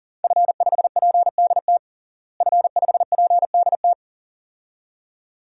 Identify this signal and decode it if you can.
Morse code, heard as a keyed tone, about 40 wpm, 700 Hz. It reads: F5PDT F5PDT
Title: DXCC-100 Call Signs - 5 Characters - Repeated 2x - 40wpm